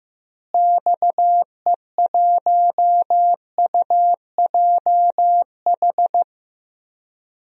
Morse code, heard as a keyed tone, about 15 wpm, 700 Hz. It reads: XE1UJH
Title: DXCC-100 Call Signs - 6 Characters - Repeated 3x - 15wpm